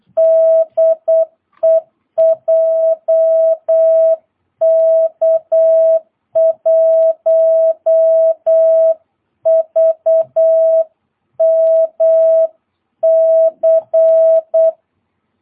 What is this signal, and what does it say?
Morse code, heard as a keyed tone, about 8 wpm, 660 Hz. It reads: DE JK1VMC
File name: jh8cbh / 2022cw